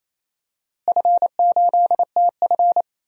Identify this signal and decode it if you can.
Morse code, heard as a keyed tone, about 28 wpm, 700 Hz. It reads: F8TF
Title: DXCC-100 Call Signs - 4 Characters - Repeated 2x - 28wpm